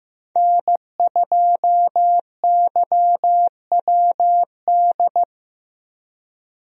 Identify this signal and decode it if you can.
Morse code, heard as a keyed tone, about 15 wpm, 700 Hz. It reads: N2YWD
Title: USA Call Signs - 5 Characters - Repeated 3x - 15wpm